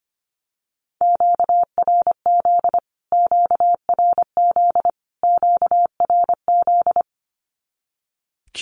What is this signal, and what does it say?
Morse code, heard as a keyed tone, about 25 wpm, 700 Hz. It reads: QRZ QRZ QRZ
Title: Q-Codes Repeated 3x 25wpm